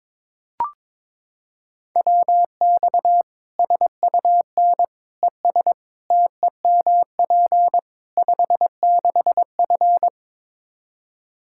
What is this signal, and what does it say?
Morse code, heard as a keyed tone, about 22 wpm, 700 Hz. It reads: WX SUN ES TEMP 56F
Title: Practice QSO - Exchange 2 - Info - 22wpm